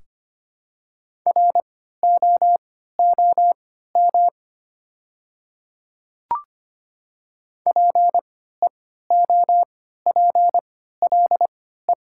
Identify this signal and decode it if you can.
Morse code, heard as a keyed tone, about 25 wpm, spaced as for 14 wpm, 700 Hz. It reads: ROOM PEOPLE
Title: 3x Character Spacing - Top 500 Words - 25wpm